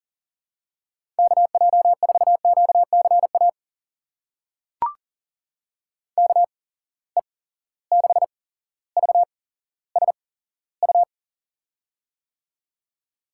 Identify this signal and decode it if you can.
Morse code, heard as a keyed tone, about 40 wpm, spaced as for 17 wpm, 700 Hz. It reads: KJ4QCA KE6VSU
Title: USA Call Signs - 6 Characters - 8x Character Spacing - 40wpm